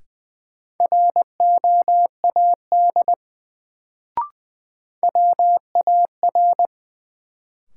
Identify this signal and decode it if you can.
Morse code, heard as a keyed tone, about 20 wpm, 700 Hz. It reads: ROAD WAR
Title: Sets of 5 Words - New Word Drill - 20wpm